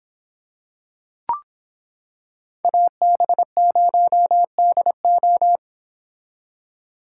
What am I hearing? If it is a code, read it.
Morse code, heard as a keyed tone, about 25 wpm, 700 Hz. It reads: AB0DO